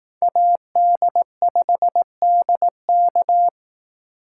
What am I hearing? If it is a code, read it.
Morse code, heard as a keyed tone, about 18 wpm, 700 Hz. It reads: AD5DK
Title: Speed Racing - USA 2x2 Call Signs 15wpm